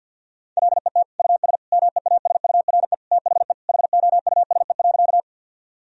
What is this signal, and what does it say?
Morse code, heard as a keyed tone, about 46 wpm, 700 Hz. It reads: LET US MEASURE THE HOUSE.
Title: Sentences from Top 1000 Words - Review Part 2 - 50wpm